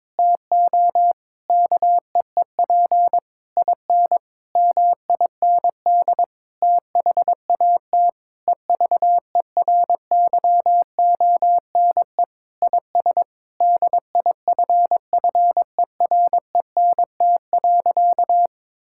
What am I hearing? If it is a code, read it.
Morse code, heard as a keyed tone, about 21 wpm, 700 Hz. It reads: TO KEEP IN MIND THAT EVERYONE IS DIFFERENT.
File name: practice / 22wpm / Sentences with 8-Word Phrases - 22wpm